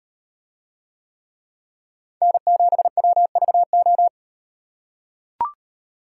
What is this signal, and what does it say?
Morse code, heard as a keyed tone, about 38 wpm, 700 Hz. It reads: N7WVO